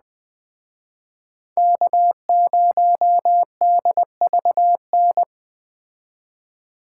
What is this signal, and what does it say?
Morse code, heard as a keyed tone, about 20 wpm, 700 Hz. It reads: K0DVN